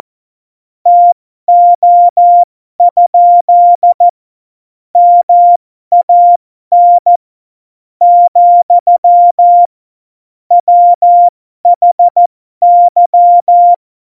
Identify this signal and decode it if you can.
Morse code, heard as a keyed tone, about 14 wpm, spaced as for 12 wpm, 700 Hz. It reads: TO? MAN , WHY
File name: Morsekurs / Witze